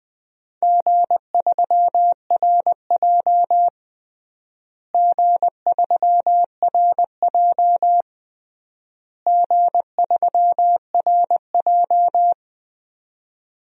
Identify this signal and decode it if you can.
Morse code, heard as a keyed tone, about 20 wpm, 700 Hz. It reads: G3RJ G3RJ G3RJ
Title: DXCC-100 Call Signs - 4 Characters - Repeated 3x - 20wpm